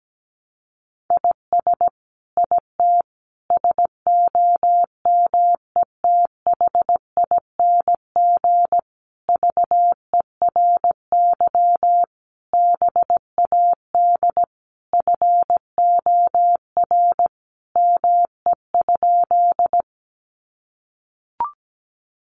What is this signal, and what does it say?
Morse code, heard as a keyed tone, about 17 wpm, 700 Hz. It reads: IS IT SOMETHING VERY BAD FOR ME?
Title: Sentences from Top 700 Words 17wpm